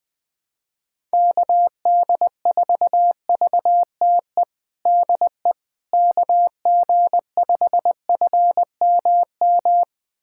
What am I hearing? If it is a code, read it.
Morse code, heard as a keyed tone, about 21 wpm, 700 Hz. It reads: KD4VTE DE KG5FMM
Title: USA Callsign DE Callsign - 6 Characters - 20wpm